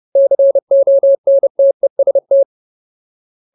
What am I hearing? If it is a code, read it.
Morse code, heard as a keyed tone, about 30 wpm, 550 Hz. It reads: CONTEST
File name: Ultimate Word List / 30WPM Ultimate